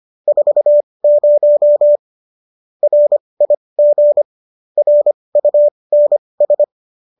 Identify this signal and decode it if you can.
Morse code, heard as a keyed tone, about 25 wpm, spaced as for 20 wpm, 590 Hz. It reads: 40 RIG RUNS
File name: qso201 20